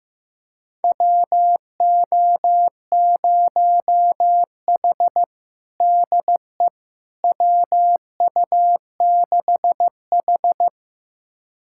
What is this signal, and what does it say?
Morse code, heard as a keyed tone, about 15 wpm, 700 Hz. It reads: WO0H DE WU6H
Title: USA Callsign DE Callsign - 4 Characters - 15wpm